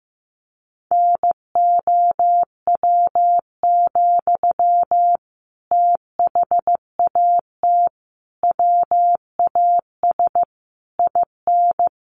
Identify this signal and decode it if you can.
Morse code, heard as a keyed tone, about 15 wpm, 700 Hz. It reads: NOW, THAT WAS IN
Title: Sentences from Top 1200 Words 15wpm